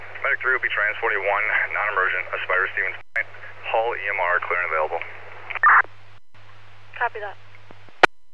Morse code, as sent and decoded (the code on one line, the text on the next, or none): none